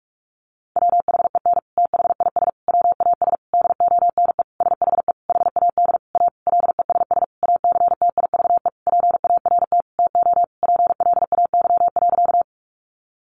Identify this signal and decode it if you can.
Morse code, heard as a keyed tone, about 44 wpm, 700 Hz. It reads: WHEN THIS WAS DONE SHE HAD A LESS ACTIVE PART TO PLAY.
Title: Sentences from Top 900 Words 45wpm